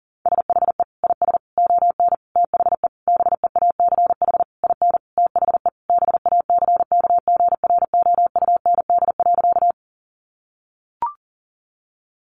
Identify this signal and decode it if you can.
Morse code, heard as a keyed tone, about 37 wpm, 700 Hz. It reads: SHE IS ON THE BEACH IN THE BACKGROUND.
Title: Sentences from Top 1500 Words 40wpm